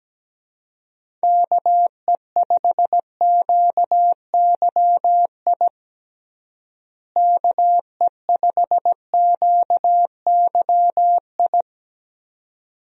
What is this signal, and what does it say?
Morse code, heard as a keyed tone, about 17 wpm, 700 Hz. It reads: KE5QYI KE5QYI
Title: USA Call Signs - 6 Characters - Repeated 2x - 17wpm